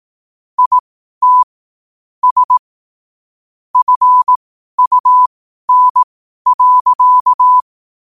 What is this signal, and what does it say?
Morse code, heard as a keyed tone, about 18 wpm, spaced as for 12 wpm, 1000 Hz. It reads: IT S FUN.
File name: qotdfiles / 12wpm